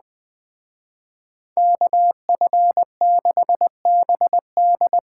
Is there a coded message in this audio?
Yes: Morse code from 1.6 to 5.0 s, heard as a keyed tone, about 21 wpm, 700 Hz. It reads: KF6BD